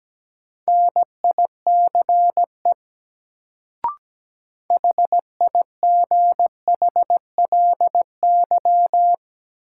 Morse code, heard as a keyed tone, about 17 wpm, 700 Hz. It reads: NICE HIGHLY
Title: Sentences from Top 1100 Words - New Word Drill Alt - 17wpm